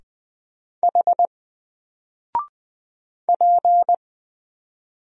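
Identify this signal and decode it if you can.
Morse code, heard as a keyed tone, about 20 wpm, 700 Hz. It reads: H P
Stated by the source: Single letter and number Morse drill at 20 wpm, from N4LCD.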